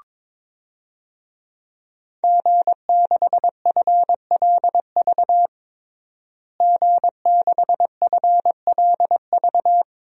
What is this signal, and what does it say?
Morse code, heard as a keyed tone, about 21 wpm, 700 Hz. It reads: G6FLV G6FLV
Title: DXCC-100 Call Signs - 5 Characters - Repeated 3x - 22wpm